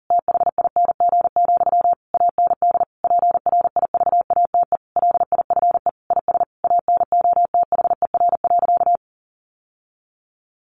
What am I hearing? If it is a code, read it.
Morse code, heard as a keyed tone, about 40 wpm, 700 Hz. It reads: THING, AND PRIVATE LIFE IS ANOTHER.
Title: Sentences from Top 1000 Words 40wpm